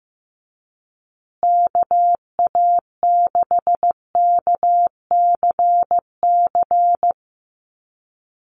Morse code, heard as a keyed tone, about 15 wpm, 700 Hz. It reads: KA6KCC
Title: USA Call Signs Repeated 2x 15wpm